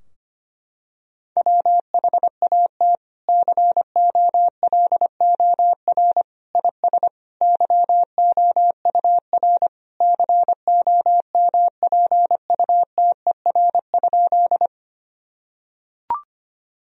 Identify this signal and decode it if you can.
Morse code, heard as a keyed tone, about 25 wpm, 700 Hz. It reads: WHAT COLOR IS YOUR COMPUTER?
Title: Sentences from Top 1000 Words - Review Part 3 - 25wpm